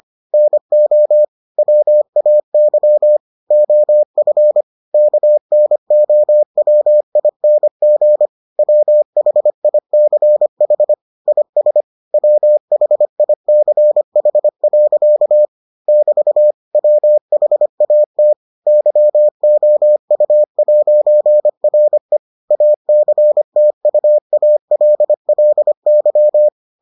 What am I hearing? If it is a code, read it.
Morse code, heard as a keyed tone, about 25 wpm, 600 Hz. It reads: NO WAY OF KNOWING WHICH IS WHICH. <BT> WHAT YOU'RE ACTUALLY